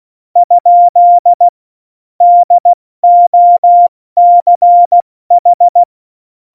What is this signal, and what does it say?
Morse code, heard as a keyed tone, about 16 wpm, spaced as for 14 wpm, 700 Hz. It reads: ? DOCH
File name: Morsekurs / Witze / Witz 14. August 2018, 70 F80